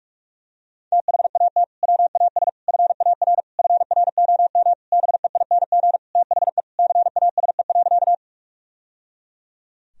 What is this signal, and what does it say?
Morse code, heard as a keyed tone, about 42 wpm, 700 Hz. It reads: THAT WAS FAR FROM BEING THE CASE.
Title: Sentences from Top 1000 Words - Review Part 3 - 45wpm